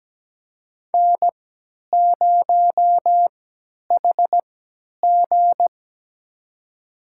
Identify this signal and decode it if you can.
Morse code, heard as a keyed tone, about 17 wpm, spaced as for 10 wpm, 700 Hz. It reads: N0HG
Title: USA Call Signs - 4 Characters - 3x Character Spacing - 17wpm